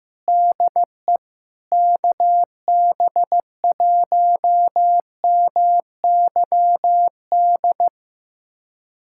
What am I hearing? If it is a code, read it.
Morse code, heard as a keyed tone, about 15 wpm, 700 Hz. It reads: DE KB1MYD
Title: USA Callsign DE Callsign - 6 Characters - 15wpm